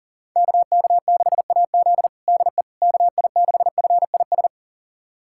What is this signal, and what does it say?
Morse code, heard as a keyed tone, about 37 wpm, 700 Hz. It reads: KK6AZ DE KI6FIS
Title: USA Callsign DE Callsign 40wpm